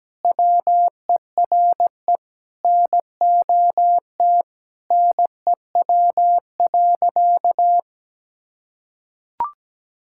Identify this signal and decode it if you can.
Morse code, heard as a keyed tone, about 17 wpm, 700 Hz. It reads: WERE NOT NEW.
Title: Sentences from Top 1600 Words 17wpm